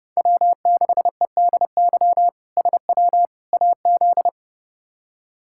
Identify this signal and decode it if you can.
Morse code, heard as a keyed tone, about 32 wpm, 700 Hz. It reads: W6EDY SW AZ
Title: K1USN SST QSO Practice - 30wpm